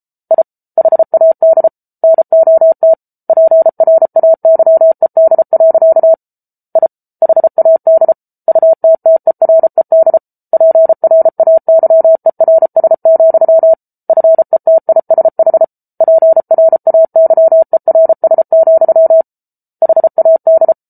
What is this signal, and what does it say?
Morse code, heard as a keyed tone, about 34 wpm, 650 Hz. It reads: I HAD NOT PRAYED. I HAD UTTERED PRAYERS, FETISH PRAYERS, HAD